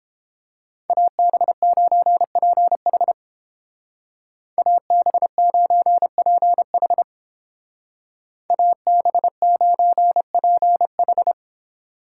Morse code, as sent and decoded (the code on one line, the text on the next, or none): .- -... ----. .--. .... / .- -... ----. .--. .... / .- -... ----. .--. ....
AB9PH AB9PH AB9PH